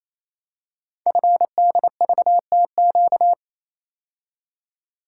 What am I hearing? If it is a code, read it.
Morse code, heard as a keyed tone, about 27 wpm, 700 Hz. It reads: FDVTQ
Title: Code Groups - 5 Letters - 28wpm